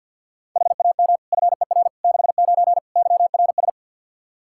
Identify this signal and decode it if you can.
Morse code, heard as a keyed tone, about 51 wpm, 700 Hz. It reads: HAM FER 69 YRS
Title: K1USN SST QSO Practice 50wpm